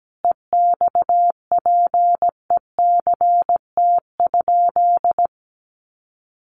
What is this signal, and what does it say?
Morse code, heard as a keyed tone, about 17 wpm, 700 Hz. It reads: EXPECT?
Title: Sentences from Top 1100 Words 17wpm